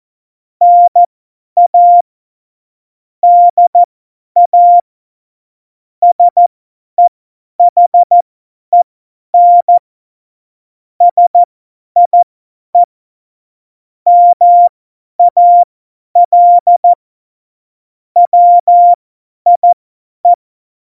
Morse code, heard as a keyed tone, about 14 wpm, spaced as for 10 wpm, 700 Hz. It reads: NA DA SEHEN SIE MAL WIE
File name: Morsekurs / Witze / Witz 09. Januar 2018, 50 F70